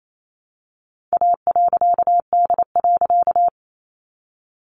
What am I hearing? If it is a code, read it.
Morse code, heard as a keyed tone, about 28 wpm, 700 Hz. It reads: A.D.
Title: A Sound Of Thunder 28wpm